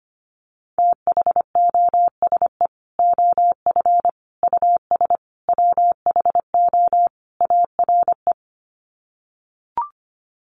Morse code, heard as a keyed tone, about 25 wpm, 700 Hz. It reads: THOSE OF US WHO ARE
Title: Sets of 5 Words Encore2 25wpm